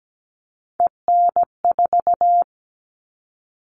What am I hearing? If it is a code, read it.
Morse code, heard as a keyed tone, about 17 wpm, 700 Hz. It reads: EN4
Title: All Cut Numbers - 3-Digit 17wpm